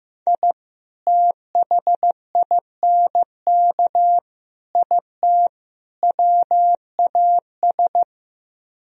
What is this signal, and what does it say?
Morse code, heard as a keyed tone, about 16 wpm, 700 Hz. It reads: I THINK IT WAS
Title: Speed-Racing - Sets of 4 Words 15wpm